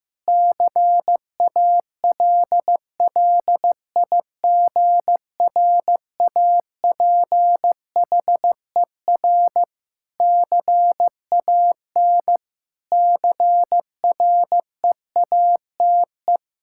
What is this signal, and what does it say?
Morse code, heard as a keyed tone, about 15 wpm, 700 Hz. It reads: CALLIGRAPHER CAN CREATE
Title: Pangram Challenge v2 - 15wpm